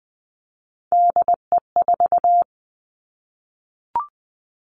Morse code, heard as a keyed tone, about 20 wpm, 700 Hz. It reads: DE4
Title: Three Challenging Letter-Number Combinations 20wpm